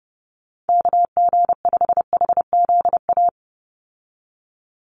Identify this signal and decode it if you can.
Morse code, heard as a keyed tone, about 29 wpm, 700 Hz. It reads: KG5HZA
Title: USA Call Signs 30wpm